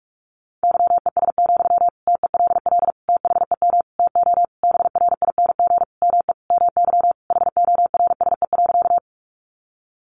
Morse code, heard as a keyed tone, about 44 wpm, 700 Hz. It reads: YES, TELL THEM TO BRING ME MY HORSE.
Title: Sentences from Top 500 Words 45wpm